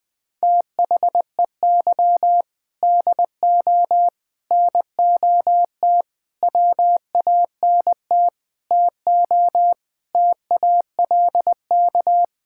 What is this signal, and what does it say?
Morse code, heard as a keyed tone, about 21 wpm, 700 Hz. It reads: THEY DO NOT WANT TO TALK
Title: Sentences with 5-Word Phrases - Spoken Beginning - 20wpm